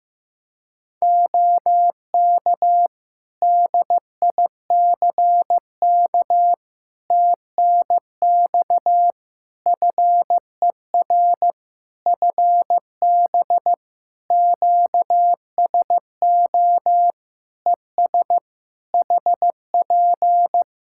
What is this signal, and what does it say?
Morse code, heard as a keyed tone, about 16 wpm, 700 Hz. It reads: OK DICK TNX FER FB QSO ES HP
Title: Practice QSO - Exchange 4 - Ending - 15wpm